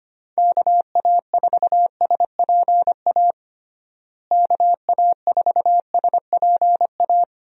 Morse code, heard as a keyed tone, about 25 wpm, 700 Hz. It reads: KA4SPA KA4SPA
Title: USA Call Signs - 6 Characters - Repeated 2x - 25wpm